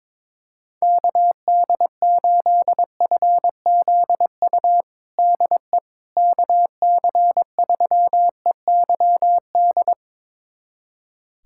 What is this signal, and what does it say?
Morse code, heard as a keyed tone, about 22 wpm, 700 Hz. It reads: KD8FZU DE KC3EYD
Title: USA Callsign DE Callsign 22wpm